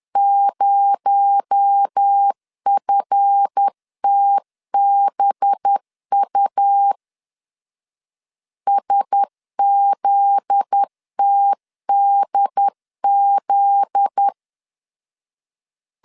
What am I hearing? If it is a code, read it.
Morse code, heard as a keyed tone, about 10 wpm, 800 Hz. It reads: 0FTBU SZTDZ
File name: LEZIONI / 7 GF9Z / 7 GF9Z 10wpm 2611